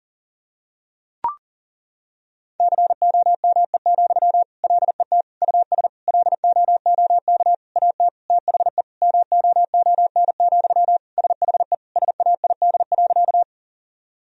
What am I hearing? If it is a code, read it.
Morse code, heard as a keyed tone, about 40 wpm, 700 Hz. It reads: COME, LET US LOOK AT THE MOON, SHE SAID.
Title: Sentences from Top 1000 Words - Review Part 3 - 40wpm